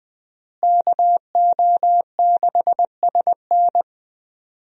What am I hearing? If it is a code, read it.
Morse code, heard as a keyed tone, about 21 wpm, 700 Hz. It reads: KO6SN
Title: USA Call Signs - 5 Characters - Repeated 3x - 20wpm